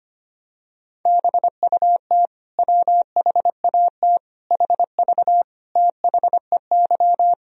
Morse code, heard as a keyed tone, about 25 wpm, 700 Hz. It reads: BUT WHAT HV THEY
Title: Sentences from Top 100 Words and Abbreviations 25wpm